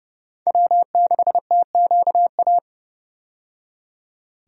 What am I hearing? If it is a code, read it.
Morse code, heard as a keyed tone, about 32 wpm, 700 Hz. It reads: W6TQA
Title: USA Call Signs - 5 Characters - 30wpm